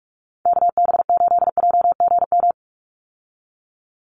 Morse code, heard as a keyed tone, about 44 wpm, 700 Hz. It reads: KB8JGM
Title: USA Call Signs 45wpm